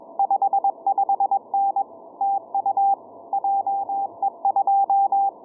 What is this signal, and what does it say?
Morse code, heard as a keyed tone, about 21 wpm, 790 Hz, with radio noise. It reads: 55N TU JE2
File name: je2pwn-g8bcg